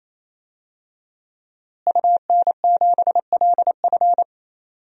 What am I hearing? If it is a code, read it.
Morse code, heard as a keyed tone, about 28 wpm, 690 Hz. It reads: UN7LF
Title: DXCC-100 Call Signs - 5 Characters - Repeated 3x - 28wpm